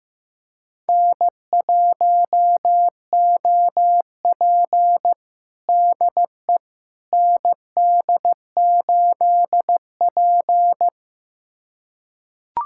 Morse code, heard as a keyed tone, about 16 wpm, 700 Hz. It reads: N1OP DE ND8P
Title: USA Callsign DE Callsign - 4 Characters - 15wpm